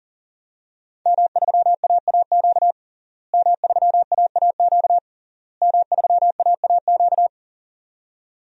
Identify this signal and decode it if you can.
Morse code, heard as a keyed tone, about 40 wpm, 700 Hz. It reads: M3AAQ M3AAQ M3AAQ
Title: DXCC-100 Call Signs - 5 Characters - Repeated 3x - 40wpm